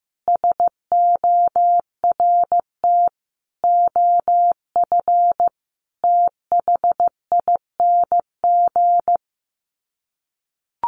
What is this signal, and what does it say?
Morse code, heard as a keyed tone, about 14 wpm, 700 Hz. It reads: SORT OF THING
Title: Sets of 5 Words Encore2 15wpm